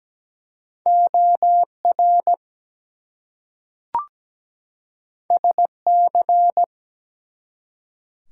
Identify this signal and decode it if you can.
Morse code, heard as a keyed tone, about 17 wpm, 700 Hz. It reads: OR SC
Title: US States and Canadian Provinces 17wpm